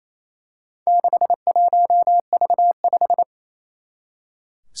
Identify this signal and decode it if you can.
Morse code, heard as a keyed tone, about 28 wpm, 700 Hz. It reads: 61V5
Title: Four Challenging Letter-Number Combinations 28wpm